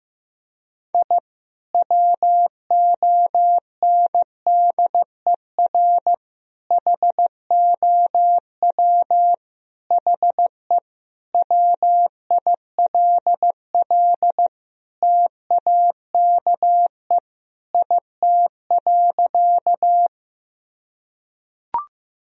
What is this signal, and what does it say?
Morse code, heard as a keyed tone, about 16 wpm, 700 Hz. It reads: I WONDER HOW HE WILL TAKE IT.
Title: Sentences from Top 1000 Words - Review Part 3 - 15wpm